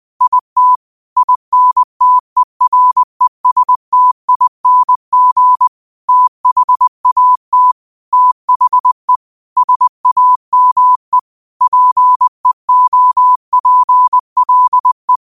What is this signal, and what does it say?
Morse code, heard as a keyed tone, about 20 wpm, 1000 Hz. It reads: IT INTERESTING THAT THE SAME PEOPLE
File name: qotdfiles / 20wpm